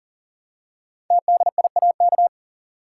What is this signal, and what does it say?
Morse code, heard as a keyed tone, about 40 wpm, 700 Hz. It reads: TDIAK
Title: Code Groups - 5 Letters - 40wpm